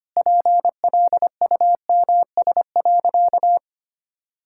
Morse code, heard as a keyed tone, about 25 wpm, 700 Hz. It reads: PLUMS.
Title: Pangram Challenge v2 - 25wpm